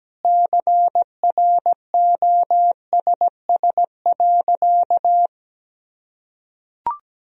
Morse code, heard as a keyed tone, about 17 wpm, 700 Hz. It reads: CROSS.
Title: Sentences from Top 1000 Words - Review Part 2 - 17wpm